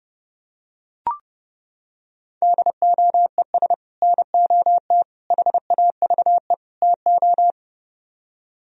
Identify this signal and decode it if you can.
Morse code, heard as a keyed tone, about 31 wpm, 700 Hz. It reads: DOES NOT HAVE TO